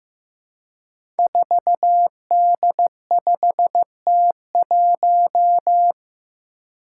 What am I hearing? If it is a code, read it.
Morse code, heard as a keyed tone, about 15 wpm, 700 Hz. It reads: 4D5T1
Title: Five Challenging Letter-Number Combinations 15wpm